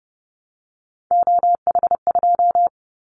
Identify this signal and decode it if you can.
Morse code, heard as a keyed tone, about 31 wpm, 700 Hz. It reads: OH2
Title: Callsign Prefixes for Top 25 DXCC - 30wpm